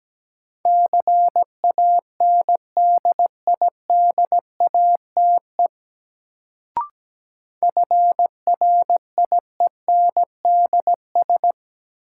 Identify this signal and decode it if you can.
Morse code, heard as a keyed tone, about 17 wpm, 700 Hz. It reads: CANDIDATE FRIENDS
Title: Sentences from Top 1600 Words - New Word Drill Alt - 17wpm